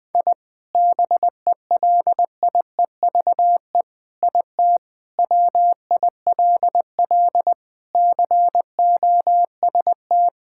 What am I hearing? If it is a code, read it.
Morse code, heard as a keyed tone, about 21 wpm, 700 Hz. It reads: I BELIEVE IT WILL COST
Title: Sentences from Top 1000 Words - Review Part 5 - 20wpm